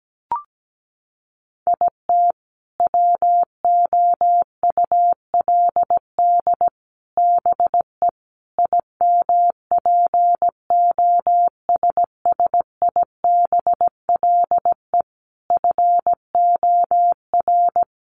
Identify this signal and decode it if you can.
Morse code, heard as a keyed tone, about 17 wpm, 700 Hz. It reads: IT WOULD BE IMPOSSIBLE FOR
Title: Sentences from Top 1100 Words 17wpm